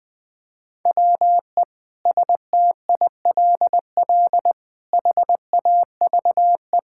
Morse code, heard as a keyed tone, about 20 wpm, 700 Hz. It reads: WE STILL HAVE
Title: Sentences with 9-Word Phrases - Spoken Beginning - 20wpm